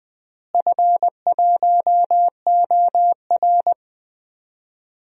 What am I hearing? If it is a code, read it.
Morse code, heard as a keyed tone, about 20 wpm, 700 Hz. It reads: F1OR
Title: DXCC-100 Call Signs - 4 Characters - Repeated 2x - 20wpm